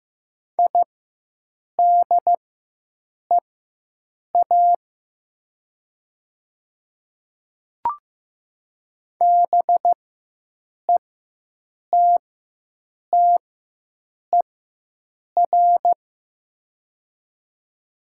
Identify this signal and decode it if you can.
Morse code, heard as a keyed tone, about 15 wpm, spaced as for 7 wpm, 700 Hz. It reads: IDEA BETTER
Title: 4x Character Spacing - Top 500 Words - 15wpm